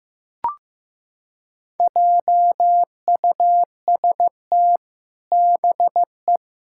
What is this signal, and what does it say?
Morse code, heard as a keyed tone, about 15 wpm, 700 Hz. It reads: JUST BE
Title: Sentences from Top 1600 Words 15wpm